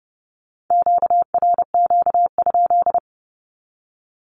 Morse code, heard as a keyed tone, about 30 wpm, 700 Hz. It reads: QRQ?
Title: Q-Codes with Spoken Meaning - 30wpm